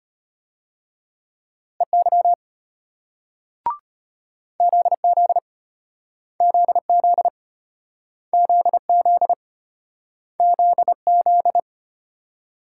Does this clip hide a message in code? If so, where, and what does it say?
Yes: Morse code from 1.8 to 11.6 s, heard as a keyed tone, about 35 wpm, 700 Hz. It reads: EY ZZ ZZ ZZ ZZ